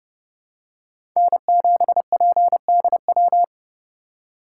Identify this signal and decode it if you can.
Morse code, heard as a keyed tone, about 30 wpm, 700 Hz. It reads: N7PDW
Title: USA Call Signs - 5 Characters - 4x Character Spacing - 30wpm